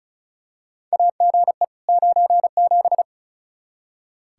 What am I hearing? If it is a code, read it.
Morse code, heard as a keyed tone, about 36 wpm, 700 Hz. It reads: AGE 97
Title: Practice QSO - Exchange 3 - More Info - 35wpm